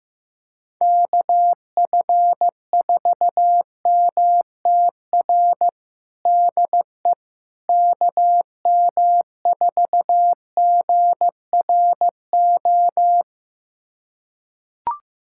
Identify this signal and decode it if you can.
Morse code, heard as a keyed tone, about 15 wpm, 700 Hz. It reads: KF4MTR DE KM4GRO
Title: USA Callsign DE Callsign 15wpm